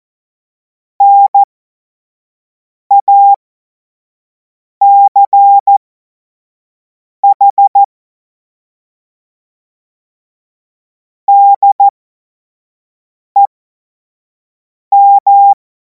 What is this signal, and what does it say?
Morse code, heard as a keyed tone, about 14 wpm, spaced as for 5 wpm, 800 Hz. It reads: NACH DEM
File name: Morsekurs / Witze / Witz 31. März 2026 25 F70